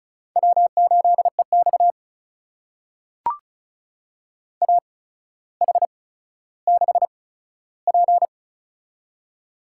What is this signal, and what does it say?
Morse code, heard as a keyed tone, about 36 wpm, spaced as for 15 wpm, 700 Hz. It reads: W8EX AH6P
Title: USA Call Signs - 4 Characters - 8x Character Spacing - 35wpm